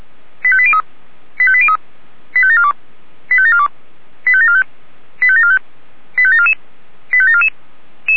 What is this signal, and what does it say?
Five-tone selective call (ZVEI), heard as quick runs of five tones, tones of about 70 ms. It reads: 85803 85803 85742 85742 85745 85745 85744 85744